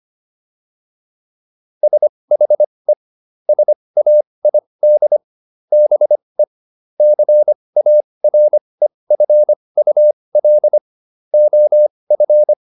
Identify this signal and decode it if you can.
Morse code, heard as a keyed tone, about 25 wpm, spaced as for 20 wpm, 600 Hz. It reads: SHE SAID BE CAREFUL OF